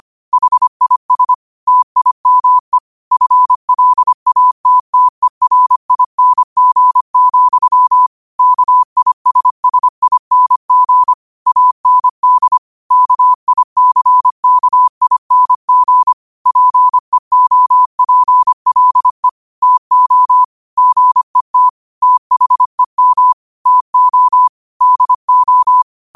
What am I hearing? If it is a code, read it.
Morse code, heard as a keyed tone, about 25 wpm, 1000 Hz. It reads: HIS TIME FLATTERING, KISSING AND KICKING PEOPLE TO GET THEM TO DO
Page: New quotes every day in morse code at 25 Words per minute.